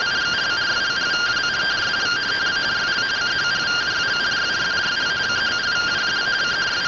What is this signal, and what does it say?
Two-tone teleprinter signal, about 170 Hz shift.
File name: findings / CT SAGITAL 27.gen-35 rtty rx